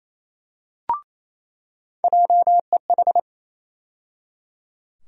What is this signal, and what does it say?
Morse code, heard as a keyed tone, about 28 wpm, 700 Hz. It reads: JEH
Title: USA Call Sign Postfixes - 3 Letters 28wpm